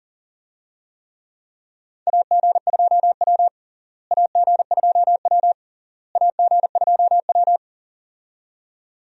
Morse code, heard as a keyed tone, about 37 wpm, 700 Hz. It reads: AG2W AG2W AG2W
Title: USA Call Signs - 4 Characters - Repeated 3x - 40wpm